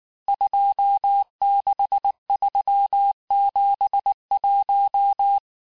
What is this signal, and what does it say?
Morse code, heard as a keyed tone, about 19 wpm, 800 Hz. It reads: 26371
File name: cours-du-lundi files / cours-1595-teste-CHIFFRE-vitesse18-MPM